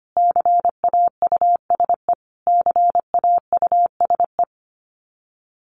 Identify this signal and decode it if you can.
Morse code, heard as a keyed tone, about 25 wpm, 700 Hz. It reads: CAUSE CAUSE
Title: Top 300 Words Repeated 25wpm